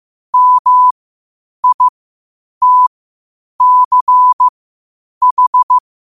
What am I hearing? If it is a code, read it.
Morse code, heard as a keyed tone, about 15 wpm, spaced as for 8 wpm, 1000 Hz. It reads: MITCH
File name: qotdfiles / 8wpm